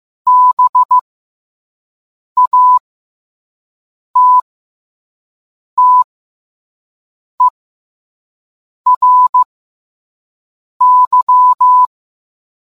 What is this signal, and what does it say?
Morse code, heard as a keyed tone, about 15 wpm, spaced as for 5 wpm, 1000 Hz. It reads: BATTERY